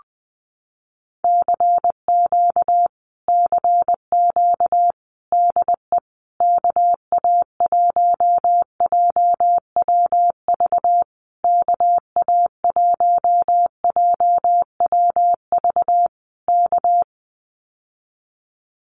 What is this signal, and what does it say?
Morse code, heard as a keyed tone, about 20 wpm, 700 Hz. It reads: CQ CQ DE KA1JWV KA1JWV K
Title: Common QSO Phrases Encore 20wpm